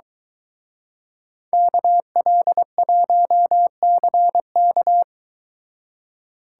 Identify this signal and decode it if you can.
Morse code, heard as a keyed tone, about 23 wpm, 700 Hz. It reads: KL1CK